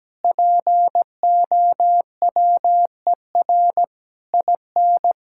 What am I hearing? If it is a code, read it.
Morse code, heard as a keyed tone, about 17 wpm, 700 Hz. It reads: POWER IN
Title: Sentences from Top 1000 Words - Review Part 6 - 17wpm